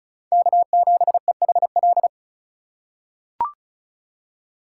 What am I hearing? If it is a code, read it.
Morse code, heard as a keyed tone, about 34 wpm, 700 Hz. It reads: K7EHL
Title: USA Call Signs - 5 Characters - 3x Character Spacing - 35wpm